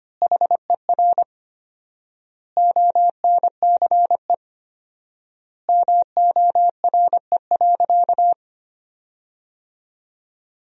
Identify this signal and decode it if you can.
Morse code, heard as a keyed tone, about 25 wpm, 700 Hz. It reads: HER ONCE MORE.
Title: 4x Word-Spacing - Sentences from Top 500 Words - 25wpm